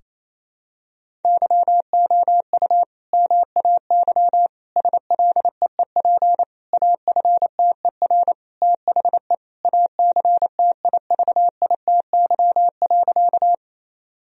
Morse code, heard as a keyed tone, about 28 wpm, 700 Hz. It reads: YOU MAY SLEEP AFTER THE ACTIVITY.
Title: Sentences from Top 1000 Words - Review Part 4 - 28wpm